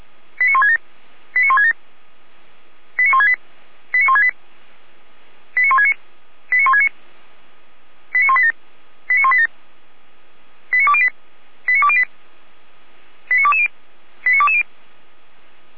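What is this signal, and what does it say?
Five-tone selective call (ZVEI), heard as quick runs of five tones, tones of about 80 ms. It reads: 89167 89167 89168 89168 89169 89169 89187 89187 89208 89208 89220 89220
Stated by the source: ZVEI five-tone selective call received on SDR.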